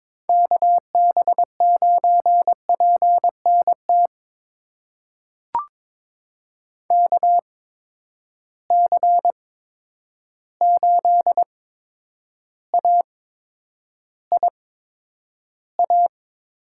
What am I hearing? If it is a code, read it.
Morse code, heard as a keyed tone, about 22 wpm, spaced as for 9 wpm, 700 Hz. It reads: KB9PNT KC8AIA
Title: USA Call Signs - 6 Characters - 8x Character Spacing - 22wpm